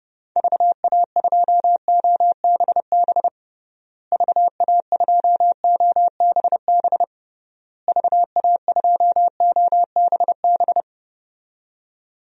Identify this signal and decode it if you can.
Morse code, heard as a keyed tone, about 30 wpm, 700 Hz. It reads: VA2OBB VA2OBB VA2OBB
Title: DXCC-100 Call Signs - 6 Characters - Repeated 3x - 30wpm